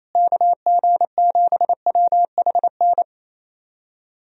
Morse code, heard as a keyed tone, about 28 wpm, 700 Hz. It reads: KG7WHN
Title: USA Call Signs - 6 Characters - Repeated 2x - 28wpm